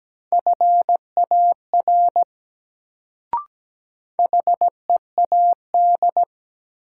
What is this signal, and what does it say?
Morse code, heard as a keyed tone, about 17 wpm, 700 Hz. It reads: FAR HEAD
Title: Sentences from Top 300 Words - New Word Drill - 17wpm